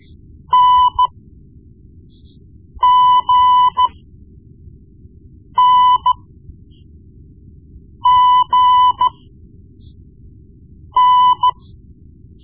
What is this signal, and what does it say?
Morse code, heard as a keyed tone, about 14 wpm, spaced as for 4 wpm, 1000 Hz. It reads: NGNGN
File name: Morse Lessons / Morse Lessons / Tape-1B